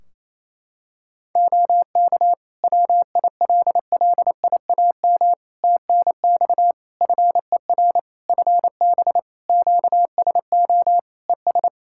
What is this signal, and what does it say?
Morse code, heard as a keyed tone, about 28 wpm, 700 Hz. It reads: OK WILLIAM TNX FER FB QSO ES
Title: Practice QSO - Exchange 4 - Ending - 28wpm